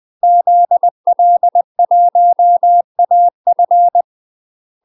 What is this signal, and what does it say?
Morse code, heard as a keyed tone, about 20 wpm, 700 Hz. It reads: ZL1AF
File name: UK MASTER QSOs TRAINING / 20WPM / UK MASTER QSOs 0031